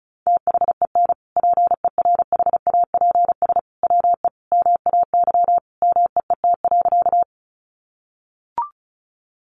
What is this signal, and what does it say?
Morse code, heard as a keyed tone, about 34 wpm, 700 Hz. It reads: THEN PERHAPS WE MAY MEET.
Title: Sentences from Top 1100 Words 35wpm